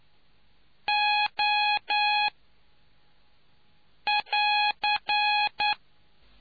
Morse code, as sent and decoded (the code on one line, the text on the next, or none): --- / .-.-.
O <AR>